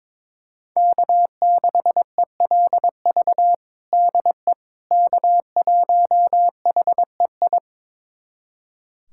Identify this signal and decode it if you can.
Morse code, heard as a keyed tone, about 22 wpm, 700 Hz. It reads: K6ELV DE K1HEI
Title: USA Callsign DE Callsign - 5 Characters - 22wpm